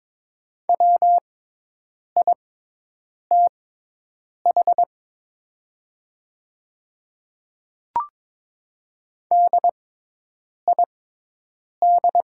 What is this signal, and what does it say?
Morse code, heard as a keyed tone, about 22 wpm, spaced as for 8 wpm, 700 Hz. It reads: WITH DID
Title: 6x Character Spacing - Top 200 Words - 22wpm